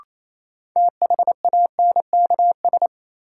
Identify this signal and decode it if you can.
Morse code, heard as a keyed tone, about 28 wpm, 700 Hz. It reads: THANKS
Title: Sets of 5 Words - New Word Drill - 28wpm